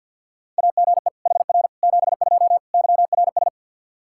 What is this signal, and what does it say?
Morse code, heard as a keyed tone, about 50 wpm, 700 Hz. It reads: AGE HR 72 YRS
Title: K1USN SST QSO Practice 50wpm